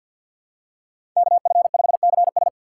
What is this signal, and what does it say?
Morse code, heard as a keyed tone, about 48 wpm, 700 Hz. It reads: KF5CS
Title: USA Call Signs - 5 Characters - 6x Character Spacing - 50wpm